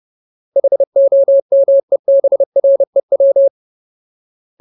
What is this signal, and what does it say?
Morse code, heard as a keyed tone, about 30 wpm, 550 Hz. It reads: HOMEBREW
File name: Ultimate Word List / 30WPM Ultimate